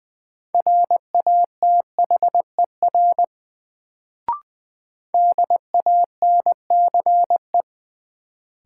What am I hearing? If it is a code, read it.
Morse code, heard as a keyed tone, about 20 wpm, 700 Hz. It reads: RATHER DANCE
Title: Sentences from Top 700 Words - New Word Drill - 20wpm